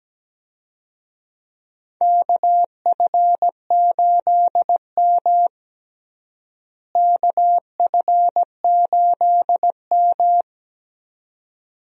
Morse code, heard as a keyed tone, about 17 wpm, 700 Hz. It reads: KF8M KF8M
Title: USA Call Signs - 4 Characters - Repeated 2x - 17wpm